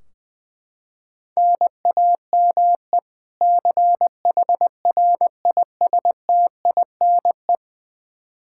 Morse code, heard as a keyed tone, about 21 wpm, 700 Hz. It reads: NAME CHRISTINE
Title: Practice QSO - Exchange 1 - Beginning - 20wpm